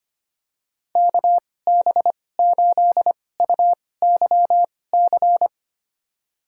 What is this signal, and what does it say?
Morse code, heard as a keyed tone, about 25 wpm, spaced as for 18 wpm, 700 Hz. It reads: KB8UYC
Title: USA Call Signs - 6 Characters - 2x Character Spacing - 25wpm